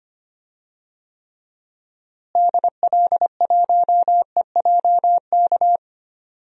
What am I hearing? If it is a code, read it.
Morse code, heard as a keyed tone, about 25 wpm, 700 Hz. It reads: DL1EJK
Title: DXCC-100 Call Signs - 6 Characters - Repeated 3x - 25wpm